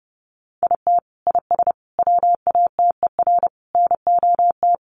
Morse code, heard as a keyed tone, about 30 wpm, 700 Hz. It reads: IT IS WATER NOT
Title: Sentences from Top 1100 Words 30wpm